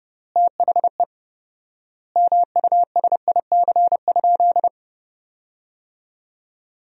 Morse code, heard as a keyed tone, about 30 wpm, 700 Hz. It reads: THE MUSIC?
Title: 4x Word-Spacing - Sentences from Top 300 Words - 30wpm